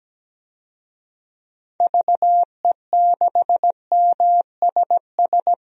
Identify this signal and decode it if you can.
Morse code, heard as a keyed tone, about 17 wpm, 700 Hz. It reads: VE6MSS
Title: DXCC-100 Call Signs - 6 Characters - 17wpm